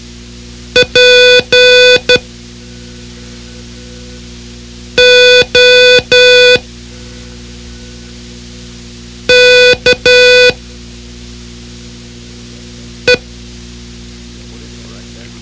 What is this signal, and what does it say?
Morse code, heard as a keyed tone, about 12 wpm, spaced as for 3 wpm, 500 Hz. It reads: POKE
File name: ham radio / CodeQuick / Code Quick 04